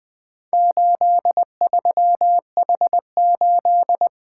Morse code, heard as a keyed tone, about 20 wpm, 700 Hz. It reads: 83H8
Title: Four Letter-Number Combinations - 20wpm